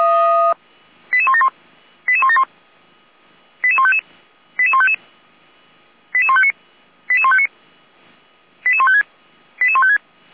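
Five-tone selective call (ZVEI), heard as quick runs of five tones, tones of about 70 ms. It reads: 88171 88171 88155 88155 88159 88159 88156 88156
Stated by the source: ZVEI five-tone selective call 88501 received on SDR, from the Feuerwehr Orlamünde audio archive.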